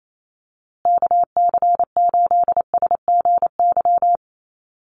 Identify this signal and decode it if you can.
Morse code, heard as a keyed tone, about 28 wpm, 700 Hz. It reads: KC8SGY
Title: USA Call Signs Repeated 2x 28wpm